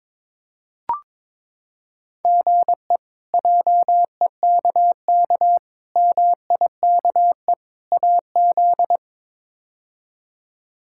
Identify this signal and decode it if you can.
Morse code, heard as a keyed tone, about 22 wpm, 700 Hz. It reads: GE JEKK MIKE AZ
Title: K1USN SST QSO Practice - 22wpm